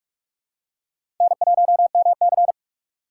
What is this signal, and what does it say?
Morse code, heard as a keyed tone, about 45 wpm, 700 Hz. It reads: N1MC
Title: USA Call Signs - 4x Character Spacing - 45wpm